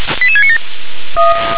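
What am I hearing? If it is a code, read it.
Five-tone selective call (ZVEI), heard as quick runs of five tones, tones of about 70 ms. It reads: 88697
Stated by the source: ZVEI five-tone selective call received on SDR.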